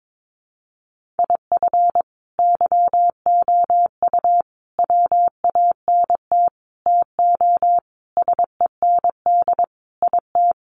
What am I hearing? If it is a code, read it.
Morse code, heard as a keyed tone, about 22 wpm, 700 Hz. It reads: IF YOU WANT TO SEND IT
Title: Sentences from Top 1500 Words 22wpm